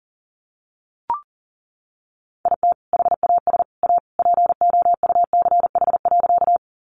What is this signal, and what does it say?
Morse code, heard as a keyed tone, about 40 wpm, 700 Hz. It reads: IT HAS A POUCH.
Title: Easy Phrases 40wpm